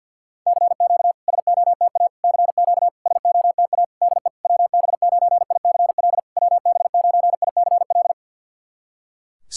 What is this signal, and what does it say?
Morse code, heard as a keyed tone, about 48 wpm, 690 Hz. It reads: CQ SOTA CQ SOTA DE WB9ICL WB9ICL
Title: SOTA QSO Practice 50wpm